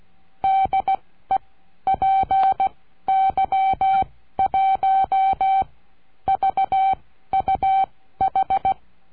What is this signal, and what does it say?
Morse code, heard as a keyed tone, about 16 wpm, spaced as for 12 wpm, 770 Hz. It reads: DE PY1VUH
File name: cursos / CURSO DE TELEGRAFIA / Telegrafia - Trilha 22